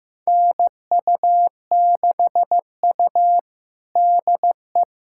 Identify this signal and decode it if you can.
Morse code, heard as a keyed tone, about 15 wpm, 700 Hz. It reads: NU6U DE
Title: USA Callsign DE Callsign - 4 Characters - 15wpm